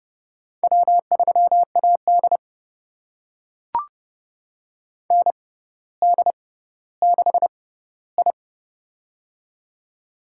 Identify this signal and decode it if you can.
Morse code, heard as a keyed tone, about 30 wpm, spaced as for 15 wpm, 700 Hz. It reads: W3AD ND6I
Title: USA Call Signs - 4 Characters - 6x Character Spacing - 30wpm